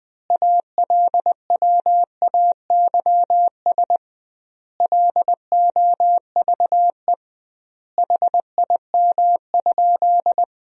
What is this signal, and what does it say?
Morse code, heard as a keyed tone, about 20 wpm, 700 Hz. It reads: ALWAYS LOVE HIM?
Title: 2x Word-Spacing - Sentences from Top 400 Words - 20wpm